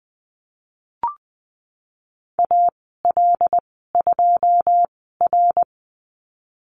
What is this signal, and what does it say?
Morse code, heard as a keyed tone, about 20 wpm, spaced as for 15 wpm, 700 Hz. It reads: AL2R
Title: 140 Lesson 23g - Letter f - Call Signs 20wpm